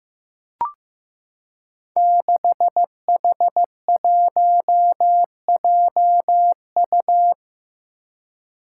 Morse code, heard as a keyed tone, about 16 wpm, 700 Hz. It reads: 6H1JU
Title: Five Challenging Letter-Number Combinations 15wpm